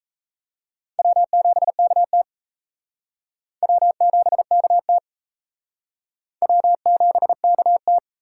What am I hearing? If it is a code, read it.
Morse code, heard as a keyed tone, about 34 wpm, 700 Hz. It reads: W7KT W7KT W7KT
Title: Speed Racing - USA 1x2 Call Signs 28wpm